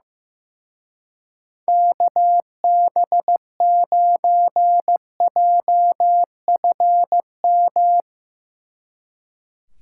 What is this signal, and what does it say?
Morse code, heard as a keyed tone, about 16 wpm, 700 Hz. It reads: KB9JFM
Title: Speed Racing - USA Call Signs 15wpm